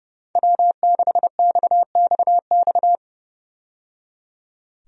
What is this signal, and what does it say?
Morse code, heard as a keyed tone, about 30 wpm, 700 Hz. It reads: W6XXX
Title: USA Call Signs - 5 Characters - 30wpm